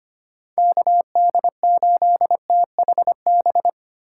Morse code, heard as a keyed tone, about 25 wpm, 700 Hz. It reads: KD8THB
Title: USA Call Signs - 6 Characters - 2x Character Spacing - 25wpm